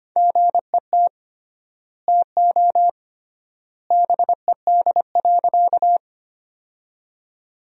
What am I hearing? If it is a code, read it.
Morse code, heard as a keyed tone, about 25 wpm, 700 Hz. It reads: GET TO BED.
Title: 3x Word-Spacing - Sentences from Top 500 Words - 25wpm